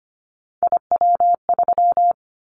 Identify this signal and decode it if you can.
Morse code, heard as a keyed tone, about 25 wpm, 700 Hz. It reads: IW3
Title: Callsign Prefixes for Top 25 DXCC - 25wpm